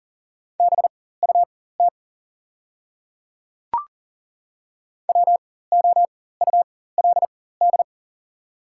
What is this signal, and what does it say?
Morse code, heard as a keyed tone, about 37 wpm, spaced as for 18 wpm, 690 Hz. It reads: BUT WOULD
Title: 4x Character Spacing - Top 100 Words - 40wpm